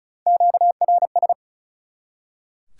Morse code, heard as a keyed tone, about 45 wpm, 700 Hz. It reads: QRS